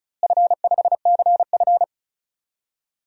Morse code, heard as a keyed tone, about 36 wpm, 700 Hz. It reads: F5CF
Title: DXCC-100 Call Signs - 4 Characters - Repeated 3x - 35wpm